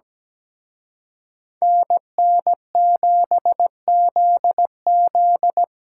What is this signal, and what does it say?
Morse code, heard as a keyed tone, about 20 wpm, 700 Hz. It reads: NN7ZZ
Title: Speed Racing - USA 2x2 Call Signs 17wpm